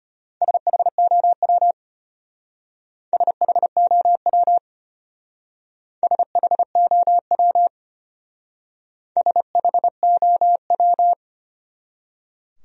Morse code, heard as a keyed tone, about 32 wpm, 700 Hz. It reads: SHOW SHOW SHOW SHOW